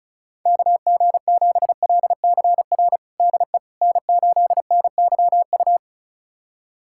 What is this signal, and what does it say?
Morse code, heard as a keyed tone, about 36 wpm, 700 Hz. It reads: KG7LCR DE N8NYU
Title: USA Callsign DE Callsign 35wpm